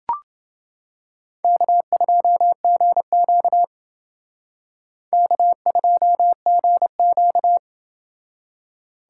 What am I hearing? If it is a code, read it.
Morse code, heard as a keyed tone, about 28 wpm, 700 Hz. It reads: K2GQ K2GQ